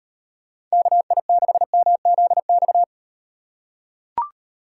Morse code, heard as a keyed tone, about 35 wpm, 700 Hz. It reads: KI6MZX